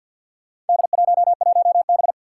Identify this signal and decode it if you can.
Morse code, heard as a keyed tone, about 51 wpm, 700 Hz. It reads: D11B
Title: Four Challenging Letter-Number Combinations 50wpm